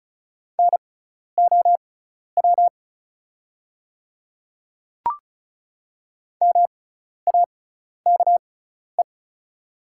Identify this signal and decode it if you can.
Morse code, heard as a keyed tone, about 34 wpm, spaced as for 12 wpm, 690 Hz. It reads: NOW MAKE
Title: 6x Character Spacing - Top 200 Words - 35wpm